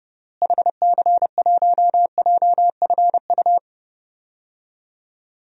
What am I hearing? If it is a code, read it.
Morse code, heard as a keyed tone, about 30 wpm, 700 Hz. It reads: HC1JFU
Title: DXCC-100 Call Signs - 6 Characters - Repeated 3x - 30wpm